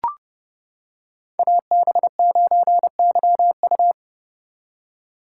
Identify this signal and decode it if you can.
Morse code, heard as a keyed tone, about 31 wpm, 700 Hz. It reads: AB9YU